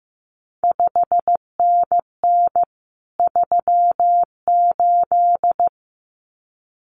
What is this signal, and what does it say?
Morse code, heard as a keyed tone, about 15 wpm, 700 Hz. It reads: 5NN 38
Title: CQ WW DX Contest Exchange Encore - 15wpm